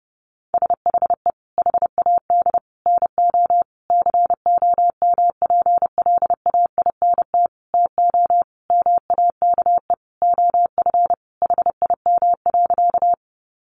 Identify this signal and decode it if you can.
Morse code, heard as a keyed tone, about 30 wpm, 700 Hz. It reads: SHE HAD NO COMPLAINT TO MAKE OF HIM.
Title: Sentences from Top 1400 Words 30wpm